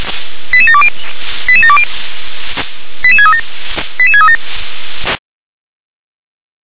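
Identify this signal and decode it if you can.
Five-tone selective call (ZVEI), heard as quick runs of five tones, tones of about 70 ms. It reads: 88720 88720 88637 88637
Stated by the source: ZVEI five-tone selective call received on an SDR.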